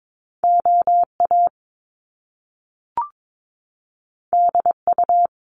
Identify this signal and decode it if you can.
Morse code, heard as a keyed tone, about 22 wpm, 700 Hz. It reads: OA DU
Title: Top Two Letter Combinations in Words 22wpm